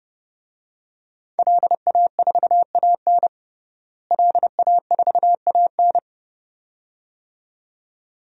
Morse code, heard as a keyed tone, about 30 wpm, 700 Hz. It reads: LA4AN LA4AN
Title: DXCC-100 Call Signs - 5 Characters - Repeated 2x - 30wpm